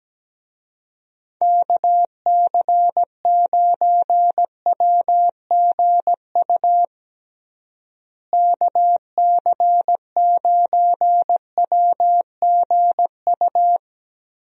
Morse code, heard as a keyed tone, about 17 wpm, 700 Hz. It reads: KC9WGU KC9WGU
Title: USA Call Signs - 6 Characters - Repeated 3x - 17wpm